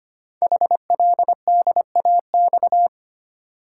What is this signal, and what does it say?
Morse code, heard as a keyed tone, about 25 wpm, 700 Hz. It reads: HLDAX
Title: Code Groups - 5 Letters - 25wpm